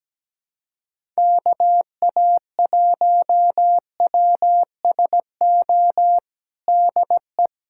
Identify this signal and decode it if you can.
Morse code, heard as a keyed tone, about 17 wpm, 700 Hz. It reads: KA1WSO DE
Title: USA Callsign DE Callsign - 6 Characters - 17wpm